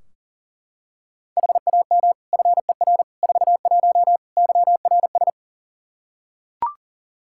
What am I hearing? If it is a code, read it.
Morse code, heard as a keyed tone, about 40 wpm, 700 Hz. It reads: HAM FER 41 YRS
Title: Practice QSO - Exchange 3 - More Info - 40wpm